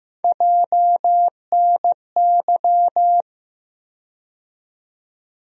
Morse code, heard as a keyed tone, about 15 wpm, 700 Hz. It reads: JNY
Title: USA Call Sign Postfixes - 3 Letters 15wpm